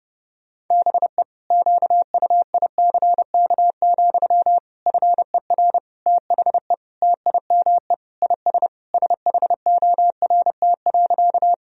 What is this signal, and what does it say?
Morse code, heard as a keyed tone, about 30 wpm, 700 Hz. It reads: BE QUICK, FER THE TIME IS SHORT.
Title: Sentences from Top 500 Words and Abbreviations 30wpm